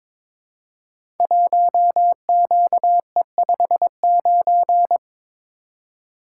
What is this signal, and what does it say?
Morse code, heard as a keyed tone, about 22 wpm, 700 Hz. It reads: 1QE59
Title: Code Groups - 5 Letter:Numbers - 22wpm